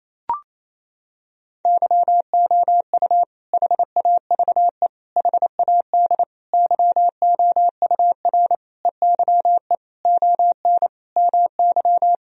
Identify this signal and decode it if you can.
Morse code, heard as a keyed tone, about 27 wpm, 700 Hz. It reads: YOU HAVE HAD YOUR EYE ON MY
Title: Sentences from Top 1000 Words - Review Part 1 - 28wpm